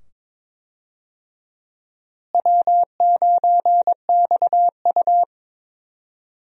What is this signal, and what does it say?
Morse code, heard as a keyed tone, about 22 wpm, 700 Hz. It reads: W9XU
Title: USA Call Signs - 4 Characters - Repeated 2x - 22wpm